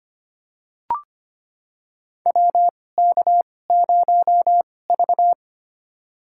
Morse code, heard as a keyed tone, about 25 wpm, spaced as for 18 wpm, 700 Hz. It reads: WK0V
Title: USA Call Signs - 2x Character Spacing - 25wpm